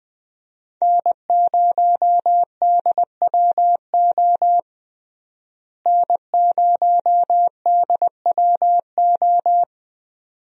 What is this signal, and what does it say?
Morse code, heard as a keyed tone, about 20 wpm, 700 Hz. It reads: N0DWO N0DWO
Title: USA Call Signs - 5 Characters - Repeated 2x - 20wpm